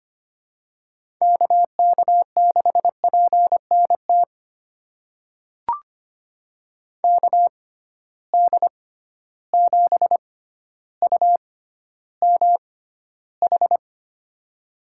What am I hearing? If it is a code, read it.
Morse code, heard as a keyed tone, about 25 wpm, spaced as for 13 wpm, 700 Hz. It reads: KK6PNT KD7UMH
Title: USA Call Signs - 6 Characters - 6x Character Spacing - 25wpm